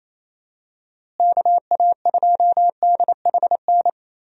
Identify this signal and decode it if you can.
Morse code, heard as a keyed tone, about 27 wpm, 700 Hz. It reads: KA2DHN
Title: USA Call Signs - 6 Characters - Repeated 2x - 28wpm